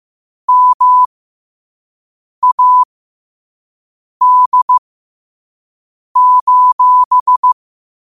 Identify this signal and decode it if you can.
Morse code, heard as a keyed tone, about 15 wpm, spaced as for 5 wpm, 1000 Hz. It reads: MAD:
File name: qotdfiles / 5wpm